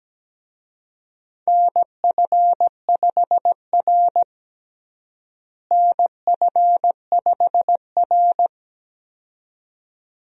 Morse code, heard as a keyed tone, about 17 wpm, 700 Hz. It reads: NF5R NF5R
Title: USA Call Signs - 4 Characters - Repeated 2x - 17wpm